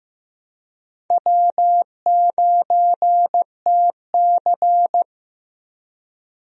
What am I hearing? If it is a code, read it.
Morse code, heard as a keyed tone, about 15 wpm, 700 Hz. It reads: W9TC
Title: USA Call Signs - 4 Characters - Repeated 2x - 15wpm